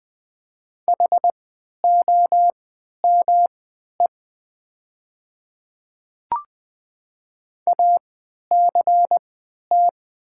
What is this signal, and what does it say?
Morse code, heard as a keyed tone, about 20 wpm, spaced as for 11 wpm, 700 Hz. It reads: HOME ACT
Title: 3x Character Spacing - Top 200 Words - 20wpm